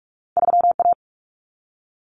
Morse code, heard as a keyed tone, about 44 wpm, 700 Hz. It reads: 3A
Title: Common Cut Numbers - 2-Digit 45wpm